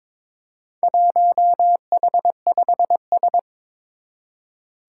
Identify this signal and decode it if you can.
Morse code, heard as a keyed tone, about 22 wpm, 700 Hz. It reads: 1H5S
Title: Four Challenging Letter-Number Combinations 22wpm